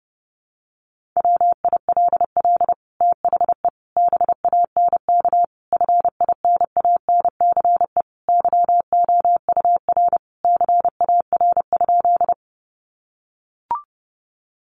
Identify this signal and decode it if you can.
Morse code, heard as a keyed tone, about 30 wpm, 700 Hz. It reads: WILL THE BANK FINANCE YOUR CAR?
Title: Sentences from Top 1500 Words 30wpm